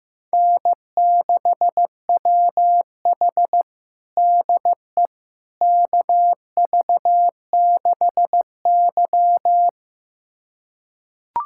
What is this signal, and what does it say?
Morse code, heard as a keyed tone, about 15 wpm, 700 Hz. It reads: N6WH DE KV6Y
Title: USA Callsign DE Callsign - 4 Characters - 15wpm